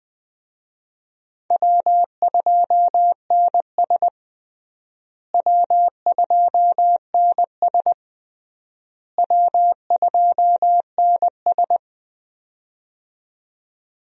Morse code, heard as a keyed tone, about 20 wpm, 700 Hz. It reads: W2NS W2NS W2NS
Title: USA Call Signs - 4 Characters - Repeated 3x - 20wpm